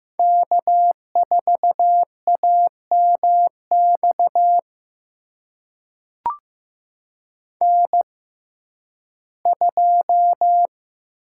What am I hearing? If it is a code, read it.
Morse code, heard as a keyed tone, about 15 wpm, spaced as for 11 wpm, 700 Hz. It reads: K4AMX N2
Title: USA Call Signs - 5 Characters - 6x Character Spacing - 15wpm